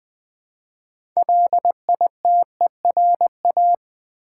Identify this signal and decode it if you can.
Morse code, heard as a keyed tone, about 20 wpm, 700 Hz. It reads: LITERA
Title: Build A Word - 2 Letters At A Time - 8-Letter Words - 20wpm